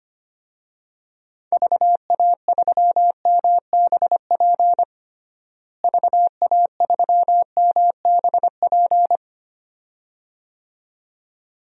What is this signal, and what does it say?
Morse code, heard as a keyed tone, about 25 wpm, 700 Hz. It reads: VA3MBP VA3MBP
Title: DXCC-100 Call Signs - 6 Characters - Repeated 2x - 25wpm